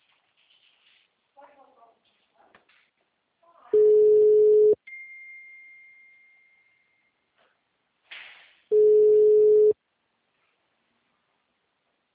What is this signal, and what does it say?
Ringback tone (Europe, 425 Hz), with speech.
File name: out- -2026- - - .15